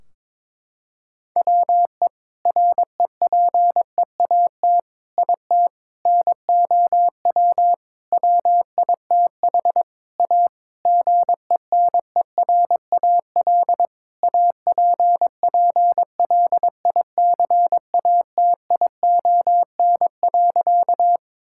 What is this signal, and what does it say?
Morse code, heard as a keyed tone, about 22 wpm, 700 Hz. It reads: WE REPEAT IT NOW WITH A GENERAL APPLICATION.
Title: Sentences from Top 1600 Words 22wpm